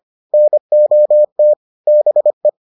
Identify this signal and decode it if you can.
Morse code, heard as a keyed tone, about 25 wpm, 600 Hz. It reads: NOT BE